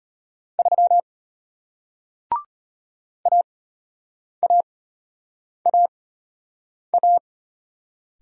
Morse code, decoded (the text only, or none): 3 AAAA